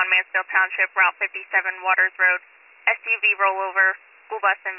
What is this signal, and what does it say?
Emergency Alert System (SAME) tones, heard as harsh buzzing bursts, with speech. It is heard from 0.3 to 4.0 s.